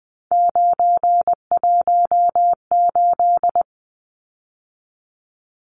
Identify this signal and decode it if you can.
Morse code, heard as a keyed tone, about 20 wpm, 700 Hz. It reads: 918
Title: Numbers - 3-Digit 20wpm